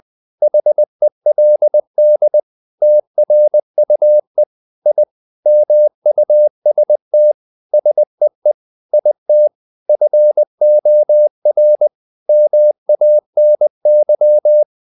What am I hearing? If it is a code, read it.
Morse code, heard as a keyed tone, about 20 wpm, 600 Hz. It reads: HELD TRUE I MUST SEE IT FOR MANY
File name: the old man and the sea / the old man and the sea part 1 20wpm